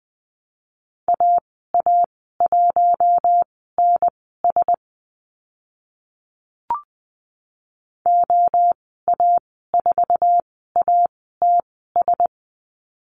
Morse code, heard as a keyed tone, about 20 wpm, spaced as for 14 wpm, 700 Hz. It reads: AA1NS OA4ATS
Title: 076 Lesson 13g - Letter L - Call Signs 20wpm